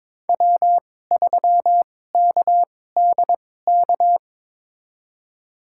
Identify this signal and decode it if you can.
Morse code, heard as a keyed tone, about 22 wpm, spaced as for 16 wpm, 700 Hz. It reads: W3KDK
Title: USA Call Signs - 5 Characters - 2x Character Spacing - 22wpm